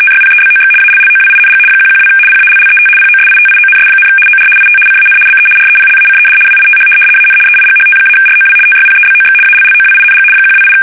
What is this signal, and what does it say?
RTTY, about 75 baud. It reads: TYPEBOX ALLOWING 112 PALLET POSITIONS, OR IT CAN BE REPLACED WITH AN EIGHT-ROW TYPEBOX ALLOWING 128
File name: fsk850-75B / teletype-corp